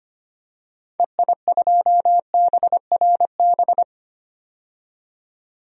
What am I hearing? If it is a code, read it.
Morse code, heard as a keyed tone, about 25 wpm, 700 Hz. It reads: EI2BRB
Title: DXCC-100 Call Signs - 6 Characters - 25wpm